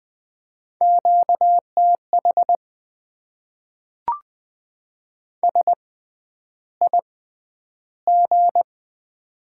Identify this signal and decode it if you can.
Morse code, heard as a keyed tone, about 20 wpm, spaced as for 10 wpm, 700 Hz. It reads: QTH SIG
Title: Top 90 QSO Elements - 6x Character Spacing - 20wpm